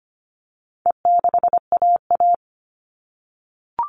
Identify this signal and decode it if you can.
Morse code, heard as a keyed tone, about 25 wpm, 700 Hz. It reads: E6AA
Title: All Cut Numbers - 4-Digit 25wpm